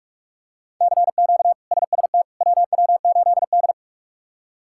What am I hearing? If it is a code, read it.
Morse code, heard as a keyed tone, about 46 wpm, 700 Hz. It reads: CQ SST WW8D
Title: K1USN SST QSO Practice - 45wpm